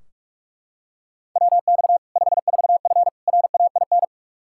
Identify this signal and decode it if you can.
Morse code, heard as a keyed tone, about 43 wpm, 700 Hz. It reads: WX 54F RAIN
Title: Practice QSO - Exchange 2 - Info - 45wpm